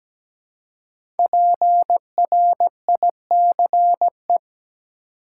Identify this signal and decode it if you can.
Morse code, heard as a keyed tone, about 17 wpm, 700 Hz. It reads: PRICE
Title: Five Letter Words - 17wpm